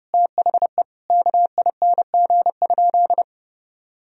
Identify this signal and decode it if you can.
Morse code, heard as a keyed tone, about 30 wpm, 700 Hz. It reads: THE KING?
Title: Sentences from Top 1000 Words - Review Part 2 - 30wpm